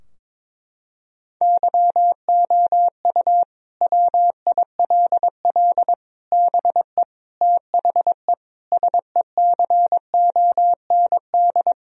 Morse code, heard as a keyed tone, about 22 wpm, 700 Hz. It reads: YOU WILL BE THE SECOND
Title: Sentences from Top 1000 Words - Review Part 4 - 22wpm